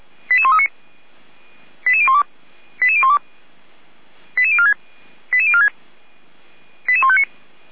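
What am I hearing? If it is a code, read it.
Five-tone selective call (ZVEI), heard as quick runs of five tones, tones of about 70 ms. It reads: 88139 88012 88012 88046 88046 88159
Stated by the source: ZVEI five-tone call received on SDR, from the feuerwehr-orlamuende.org fire brigade (f85360) audio server.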